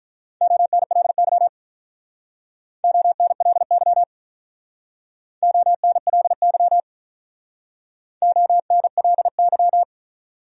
Morse code, heard as a keyed tone, about 43 wpm, 700 Hz. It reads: ONLY ONLY ONLY ONLY
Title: Speed-Racing Top 300 Words 35wpm